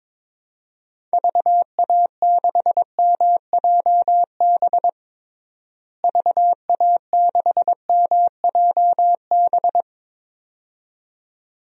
Morse code, heard as a keyed tone, about 22 wpm, 700 Hz. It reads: VA6MJB VA6MJB
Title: DXCC-100 Call Signs - 6 Characters - Repeated 2x - 22wpm